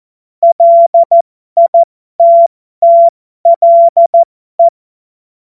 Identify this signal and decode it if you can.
Morse code, heard as a keyed tone, about 14 wpm, spaced as for 12 wpm, 680 Hz. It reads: LITTLE
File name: Morsekurs / Witze